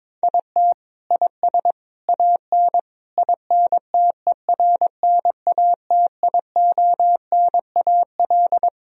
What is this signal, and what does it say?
Morse code, heard as a keyed tone, about 22 wpm, 700 Hz. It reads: IT IS AN INTERNATIONAL
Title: Sentences from Top 1000 Words - Review Part 4 - 22wpm